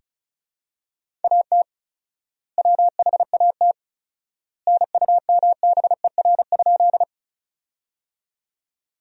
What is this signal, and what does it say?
Morse code, heard as a keyed tone, about 34 wpm, 700 Hz. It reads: AT WHAT NUMBER?
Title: 4x Word-Spacing - Sentences from Top 100 Words - 35wpm